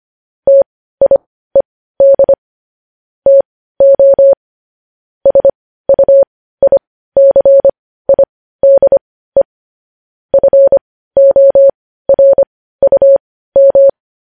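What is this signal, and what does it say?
Morse code, heard as a keyed tone, about 25 wpm, spaced as for 15 wpm, 550 Hz. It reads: TIED TO SUICIDE FORUM
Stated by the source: Morse code sending of news headlines at 15 wpm.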